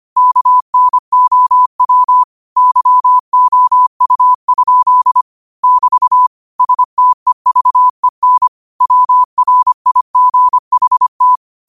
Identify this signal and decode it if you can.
Morse code, heard as a keyed tone, about 25 wpm, 1000 Hz. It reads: KNOW YOU? <BT> STEVEN WRIGHT
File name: qotdfiles / 25wpm